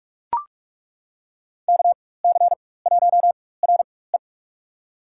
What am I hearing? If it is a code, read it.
Morse code, heard as a keyed tone, about 42 wpm, spaced as for 21 wpm, 700 Hz. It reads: KC1RE
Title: USA Call Signs - 5 Characters - 4x Character Spacing - 45wpm